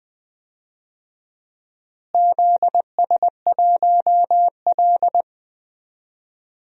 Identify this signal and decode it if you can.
Morse code, heard as a keyed tone, about 20 wpm, 700 Hz. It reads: ZS1L
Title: DXCC-100 Call Signs - 4 Characters - Repeated 3x - 20wpm